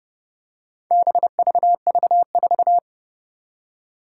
Morse code, heard as a keyed tone, about 29 wpm, 700 Hz. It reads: BVV4
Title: Four Challenging Letter-Number Combinations 30wpm